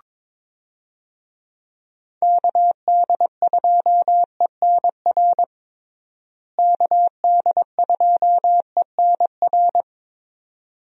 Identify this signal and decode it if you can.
Morse code, heard as a keyed tone, about 22 wpm, 700 Hz. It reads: KD2ENR KD2ENR
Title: USA Call Signs - 6 Characters - Repeated 2x - 22wpm